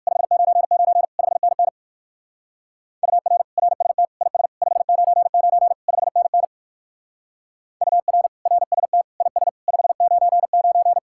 Morse code, heard as a keyed tone, about 59 wpm, 700 Hz. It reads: 599 5NN UR RST IS 599 5NN UR RST IS 599